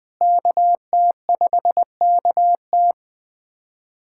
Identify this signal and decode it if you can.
Morse code, heard as a keyed tone, about 21 wpm, 700 Hz. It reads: KT5KT
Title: USA Callsign DE Callsign - 5 Characters - 20wpm